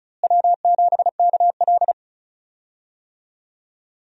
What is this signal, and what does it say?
Morse code, heard as a keyed tone, about 34 wpm, 700 Hz. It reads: W7KL
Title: USA Call Signs - 4 Characters - Repeated 2x - 35wpm